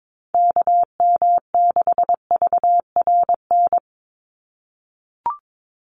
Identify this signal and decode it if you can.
Morse code, heard as a keyed tone, about 22 wpm, 700 Hz. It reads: KM6VRN
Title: USA Call Signs 22wpm